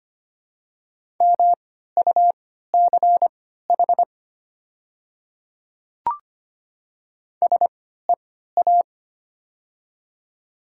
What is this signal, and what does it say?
Morse code, heard as a keyed tone, about 25 wpm, spaced as for 14 wpm, 700 Hz. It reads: MUCH SEA
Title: 3x Character Spacing - Top 300 Words - 25wpm